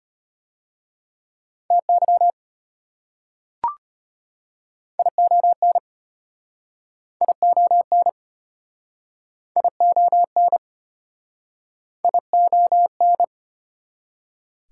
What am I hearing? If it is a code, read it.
Morse code, heard as a keyed tone, about 33 wpm, 700 Hz. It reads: TY ION ION ION ION